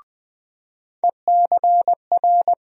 Morse code, heard as a keyed tone, about 20 wpm, 700 Hz. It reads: ECR
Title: USA Call Sign Postfixes - 3 Letters 20wpm